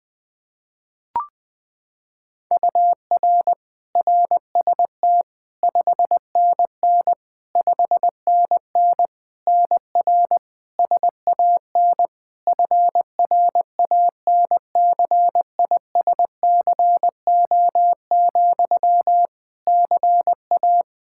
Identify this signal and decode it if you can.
Morse code, heard as a keyed tone, about 20 wpm, 700 Hz. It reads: UR RST 5NN 5NN NR SAN FRANCISCO, CA
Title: Practice QSO - Exchange 1 - Beginning - 20wpm